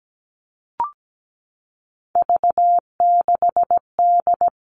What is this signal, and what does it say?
Morse code, heard as a keyed tone, about 17 wpm, 700 Hz. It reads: V6D
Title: All Cut Numbers - 3-Digit 17wpm